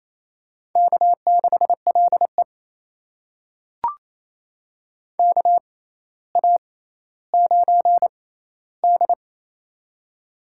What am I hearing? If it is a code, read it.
Morse code, heard as a keyed tone, about 27 wpm, spaced as for 14 wpm, 700 Hz. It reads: K6LE KA9D
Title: USA Call Signs - 4 Characters - 6x Character Spacing - 28wpm